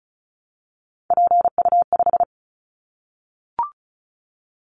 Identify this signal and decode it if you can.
Morse code, heard as a keyed tone, about 34 wpm, 700 Hz. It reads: PU5
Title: Callsign Prefixes for Top 50 DXCC - 35wpm